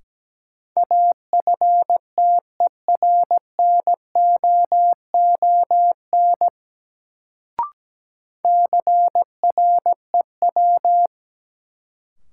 Morse code, heard as a keyed tone, about 17 wpm, 700 Hz. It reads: AFTERNOON CREW
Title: Sentences from Top 1500 Words - New Word Drill - 17wpm